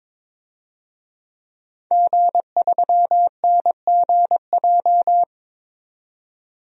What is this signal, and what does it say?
Morse code, heard as a keyed tone, about 22 wpm, 700 Hz. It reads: G3NGJ
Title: DXCC-100 Call Signs - 5 Characters - 22wpm